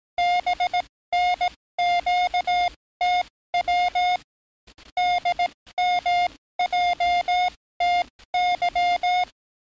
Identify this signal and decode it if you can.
Morse code, heard as a keyed tone, about 18 wpm, spaced as for 14 wpm, 710 Hz. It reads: BNQTW DMJTY
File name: training / NAVPERS 11049to11060 / NAVPERS 11057 RA RB